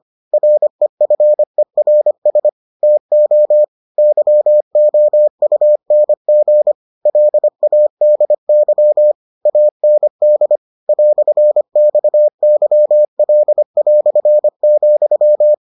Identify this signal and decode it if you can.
Morse code, heard as a keyed tone, about 25 wpm, 600 Hz. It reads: REFERS TO YOUNG LADY AND "XYL",